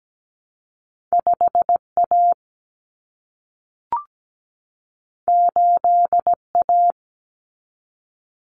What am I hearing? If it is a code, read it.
Morse code, heard as a keyed tone, about 17 wpm, 700 Hz. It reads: 5A 8A
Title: Common Cut Numbers - 2-Digit 17wpm